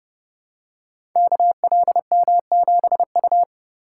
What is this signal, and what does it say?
Morse code, heard as a keyed tone, about 31 wpm, 700 Hz. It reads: KLM7U
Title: Code Groups - 5 Letter:Numbers - 30wpm